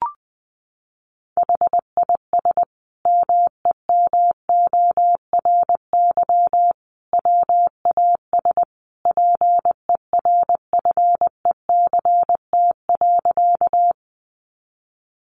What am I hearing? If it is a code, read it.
Morse code, heard as a keyed tone, about 19 wpm, 700 Hz. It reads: HIS MEMORY WAS PERFECT.
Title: Sentences from Top 1300 Words 20wpm